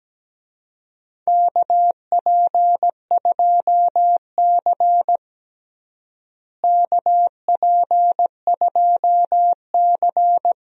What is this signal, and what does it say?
Morse code, heard as a keyed tone, about 17 wpm, 700 Hz. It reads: KP2C KP2C
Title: DXCC-100 Call Signs - 4 Characters - Repeated 3x - 17wpm